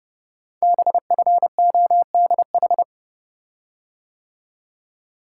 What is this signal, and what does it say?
Morse code, heard as a keyed tone, about 30 wpm, 700 Hz. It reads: BFODH
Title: Code Groups - 5 Letters - 30wpm